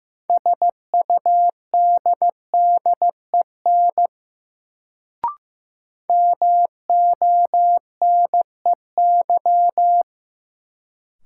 Morse code, heard as a keyed tone, about 15 wpm, 700 Hz. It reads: SUDDEN MONEY
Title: Sets of 4 Words - New Word Drill - 15wpm